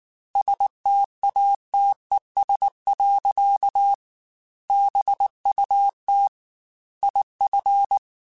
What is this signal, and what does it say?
Morse code, heard as a keyed tone, about 19 wpm, 800 Hz. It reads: STATES. BUT IF
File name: Technical Subjects / Technical Subjects-mp3s / AND Gate @18